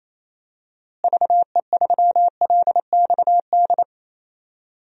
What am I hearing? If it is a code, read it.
Morse code, heard as a keyed tone, about 28 wpm, 700 Hz. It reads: VE3LXD
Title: DXCC-100 Call Signs - 6 Characters - Repeated 3x - 28wpm